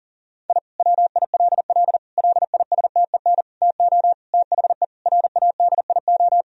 Morse code, heard as a keyed tone, about 40 wpm, 700 Hz. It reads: I WILL LISTEN TO THE RADIO
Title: Sentences from Top 1000 Words - Review Part 3 - 40wpm